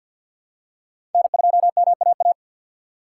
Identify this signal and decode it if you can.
Morse code, heard as a keyed tone, about 47 wpm, 700 Hz. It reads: N2MAA
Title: USA Call Signs - 6x Character Spacing - 50wpm